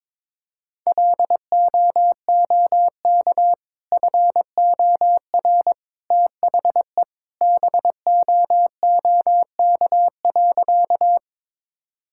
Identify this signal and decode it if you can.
Morse code, heard as a keyed tone, about 22 wpm, 700 Hz. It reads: LOOK FOR THE BOOK.
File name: practice / 22wpm / Sentences with 2-Word Phrases - 22wpm